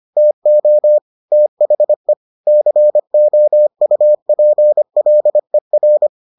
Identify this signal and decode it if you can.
Morse code, heard as a keyed tone, about 25 wpm, 600 Hz. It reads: TO THE COUPLER